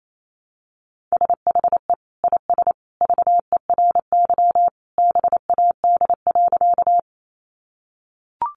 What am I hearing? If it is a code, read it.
Morse code, heard as a keyed tone, about 28 wpm, 700 Hz. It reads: SHE IS VERY BAD.
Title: Sentences from Top 700 Words 28wpm